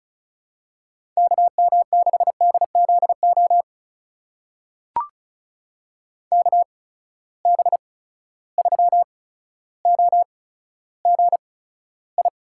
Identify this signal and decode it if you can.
Morse code, heard as a keyed tone, about 36 wpm, spaced as for 15 wpm, 700 Hz. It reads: KM6DZO KB3OGI
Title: USA Call Signs - 6 Characters - 8x Character Spacing - 35wpm